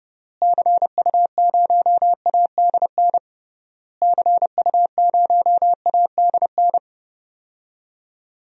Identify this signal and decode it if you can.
Morse code, heard as a keyed tone, about 30 wpm, 700 Hz. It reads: CU0ADN CU0ADN
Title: DXCC-100 Call Signs - 6 Characters - Repeated 3x - 30wpm